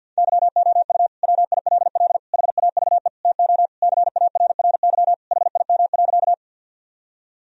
Morse code, heard as a keyed tone, about 47 wpm, 700 Hz. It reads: YOU WILL HAVE TO CARRY HIM.
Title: Sentences from Top 1000 Words - Review Part 1 - 50wpm